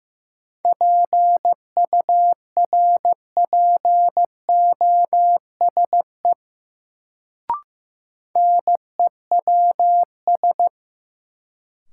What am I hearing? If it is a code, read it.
Morse code, heard as a keyed tone, about 15 wpm, 700 Hz. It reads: PURPOSE NEWS
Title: Sets of 5 Words - New Word Drill - 15wpm